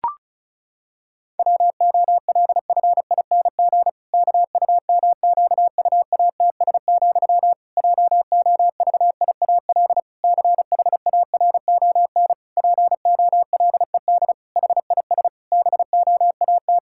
Morse code, heard as a keyed tone, about 36 wpm, 700 Hz. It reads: WOLFING KUMQUATS, JOVIAL CHARON POLED HIS BOAT
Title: Pangram Challenge v2 - 35wpm